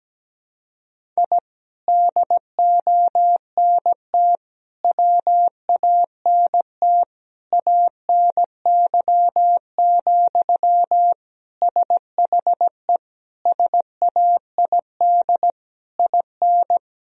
Morse code, heard as a keyed tone, about 17 wpm, 700 Hz. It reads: I DONT WANT ANY, SHE SAID IN
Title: Sentences from Top 1000 Words - Review Part 1 - 17wpm